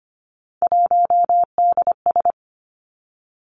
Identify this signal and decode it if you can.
Morse code, heard as a keyed tone, about 25 wpm, 700 Hz. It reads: 1DS
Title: Three Challenging Letter-Number Combinations 25wpm